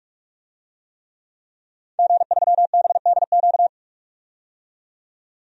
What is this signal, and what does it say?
Morse code, heard as a keyed tone, about 46 wpm, 700 Hz. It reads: G3BDQ
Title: DXCC-100 Call Signs - 5 Characters - 45wpm